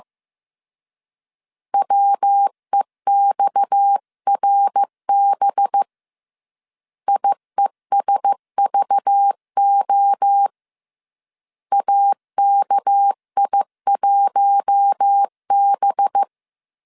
Morse code, heard as a keyed tone, about 15 wpm, 800 Hz. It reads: WEXRB IESVO AKI1B
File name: CW / DET / 1 6 1500C S B 14wpm 2958